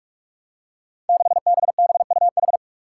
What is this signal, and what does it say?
Morse code, heard as a keyed tone, about 45 wpm, 700 Hz. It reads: 6BBUH
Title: Five Challenging Letter-Number Combinations 45wpm